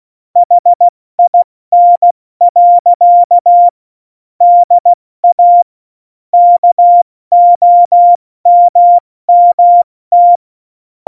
Morse code, heard as a keyed tone, about 16 wpm, spaced as for 14 wpm, 700 Hz. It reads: HIN. DA KOMMT
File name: Morsekurs / Witze / Witz 08. Dezember 2020, 70 F80